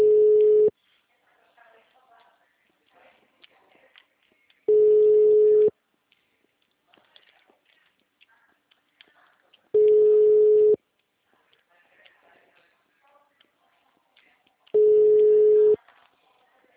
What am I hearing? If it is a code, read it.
Ringback tone (Europe, 425 Hz), with speech.